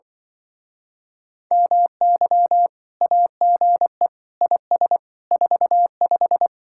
Morse code, heard as a keyed tone, about 24 wpm, 700 Hz. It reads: MY AGE IS 45
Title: Speed-Racing Common QSO Phrases Encore - 20wpm